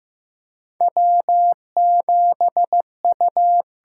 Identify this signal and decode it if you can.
Morse code, heard as a keyed tone, about 15 wpm, 700 Hz. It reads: W7U
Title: USA Call Signs - 4 Characters - 6x Character Spacing - 15wpm